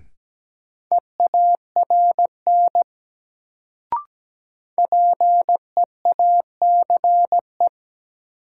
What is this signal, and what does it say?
Morse code, heard as a keyed tone, about 17 wpm, 700 Hz. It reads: EARN PEACE
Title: Sentences from Top 1500 Words - New Word Drill - 17wpm